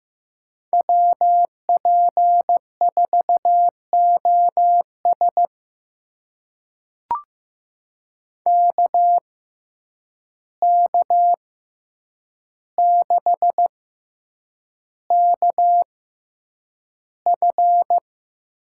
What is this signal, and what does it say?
Morse code, heard as a keyed tone, about 15 wpm, spaced as for 8 wpm, 700 Hz. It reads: WP4OS KK6KF
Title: USA Call Signs - 5 Characters - 6x Character Spacing - 15wpm